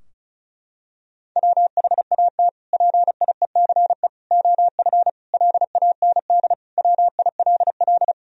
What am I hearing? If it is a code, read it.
Morse code, heard as a keyed tone, about 34 wpm, 700 Hz. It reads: WHAT PIECE OF LAND WILL
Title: Sentences from Top 1000 Words - Review Part 2 - 35wpm